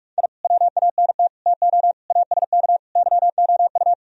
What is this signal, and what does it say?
Morse code, heard as a keyed tone, about 47 wpm, 700 Hz. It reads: I WANT TO ASK YOU